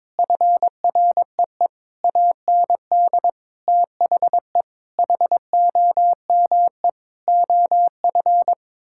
Morse code, heard as a keyed tone, about 22 wpm, 700 Hz. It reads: FREE AND THE HOME OF
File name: Sets of 9 Words - 22wpm